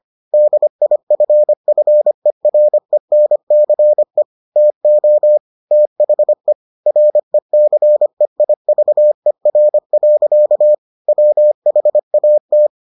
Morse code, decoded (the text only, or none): DIFFERENCE TO THE RECEIVER. WHAT